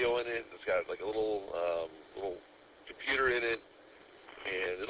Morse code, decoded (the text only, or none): none